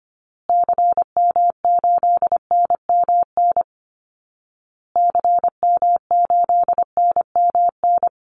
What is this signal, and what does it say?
Morse code, heard as a keyed tone, about 25 wpm, 700 Hz. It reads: CM8NMN CM8NMN
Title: Call Signs Repeated 2x - 25wpm